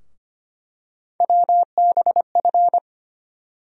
Morse code, heard as a keyed tone, about 25 wpm, 700 Hz. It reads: WBF
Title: USA Call Sign Postfixes - 3 Letters 25wpm